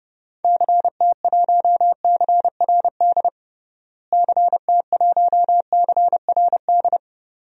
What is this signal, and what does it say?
Morse code, heard as a keyed tone, about 32 wpm, 700 Hz. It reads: CT1CRD CT1CRD
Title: DXCC-100 Call Signs - 6 Characters - Repeated 2x - 30wpm